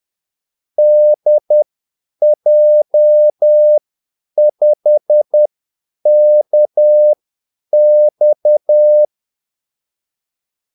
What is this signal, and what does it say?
Morse code, heard as a keyed tone, about 10 wpm, spaced as for 8 wpm, 600 Hz. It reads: DJ5KX
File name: Callsigns-10-8wpm